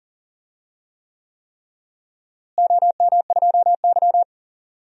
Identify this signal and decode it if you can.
Morse code, heard as a keyed tone, about 40 wpm, 700 Hz. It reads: OM2Y
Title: DXCC-100 Call Signs - 4 Characters - Repeated 3x - 40wpm